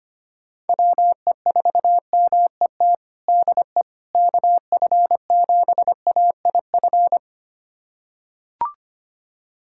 Morse code, heard as a keyed tone, about 25 wpm, 700 Hz. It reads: WE4MET DE KF7AIF
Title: USA Callsign DE Callsign - 6 Characters - 25wpm